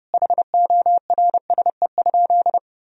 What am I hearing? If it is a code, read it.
Morse code, heard as a keyed tone, about 30 wpm, 700 Hz. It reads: HORSE?
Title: Sentences from Top 1000 Words - Review Part 1 - 30wpm